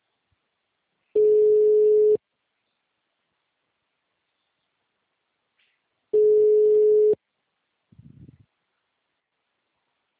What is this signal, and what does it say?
Ringback tone (Europe, 425 Hz).
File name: out- -2021- - - .501